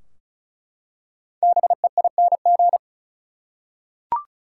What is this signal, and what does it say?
Morse code, heard as a keyed tone, about 36 wpm, 700 Hz. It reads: BEING
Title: Five Letter Words - 35wpm